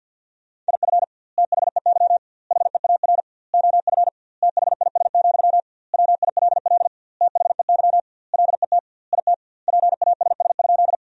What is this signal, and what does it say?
Morse code, heard as a keyed tone, about 50 wpm, 700 Hz. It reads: IF THEY HEAR OF THIS, WILL THEY LET IT PASS?
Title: 2x Word-Spacing - Sentences from Top 400 Words - 50wpm